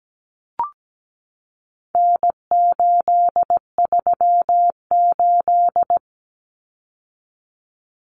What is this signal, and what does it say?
Morse code, heard as a keyed tone, about 17 wpm, 700 Hz. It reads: N838
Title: Common Cut Numbers - 4-Digit 17wpm